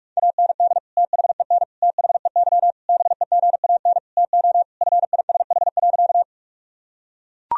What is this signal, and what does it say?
Morse code, heard as a keyed tone, about 41 wpm, 700 Hz. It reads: AND THEN THEY BEGAN TO FISH.
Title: Sentences from Top 1000 Words - Review Part 1 - 45wpm